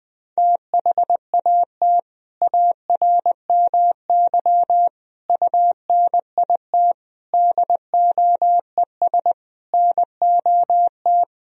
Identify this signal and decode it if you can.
Morse code, heard as a keyed tone, about 20 wpm, 700 Hz. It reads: THAT ARMY UNIT DOES NOT
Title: Sentences from Top 1000 Words - Review Part 2 - 20wpm